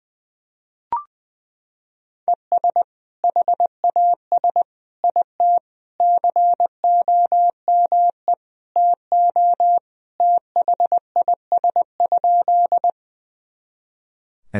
Morse code, heard as a keyed tone, about 21 wpm, 700 Hz. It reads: ES HAS IT COME TO THIS?
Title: Sentences from Top 100 Words and Abbreviations 20wpm